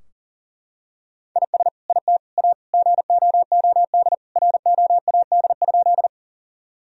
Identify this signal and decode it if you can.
Morse code, heard as a keyed tone, about 40 wpm, 700 Hz. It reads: IS IT A GOOD ROAD?
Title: Sentences from Top 1000 Words - Review Part 2 - 40wpm